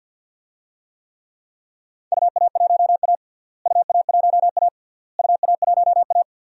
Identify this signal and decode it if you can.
Morse code, heard as a keyed tone, about 52 wpm, 700 Hz. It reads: UA1A UA1A UA1A
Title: DXCC-100 Call Signs - 4 Characters - Repeated 3x - 50wpm